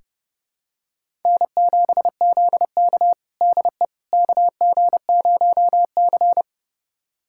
Morse code, heard as a keyed tone, about 30 wpm, 700 Hz. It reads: N7ZK DE KG0C
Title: USA Callsign DE Callsign - 4 Characters - 30wpm